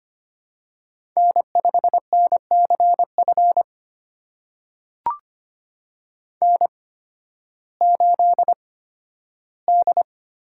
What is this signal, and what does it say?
Morse code, heard as a keyed tone, about 25 wpm, spaced as for 13 wpm, 700 Hz. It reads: N5NCF N8D
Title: USA Call Signs - 5 Characters - 8x Character Spacing - 25wpm